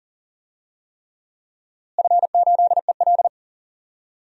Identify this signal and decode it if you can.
Morse code, heard as a keyed tone, about 40 wpm, 700 Hz. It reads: F8EL
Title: DXCC-100 Call Signs - 4 Characters - Repeated 3x - 40wpm